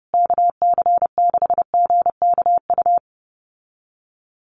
Morse code, heard as a keyed tone, about 30 wpm, 700 Hz. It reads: KC6GKU
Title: USA Call Signs Repeated 3x 30wpm